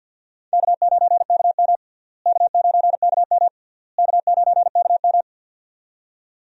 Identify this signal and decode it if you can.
Morse code, heard as a keyed tone, about 47 wpm, 700 Hz. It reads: K9KM K9KM K9KM
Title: USA Call Signs - 4 Characters - Repeated 3x - 50wpm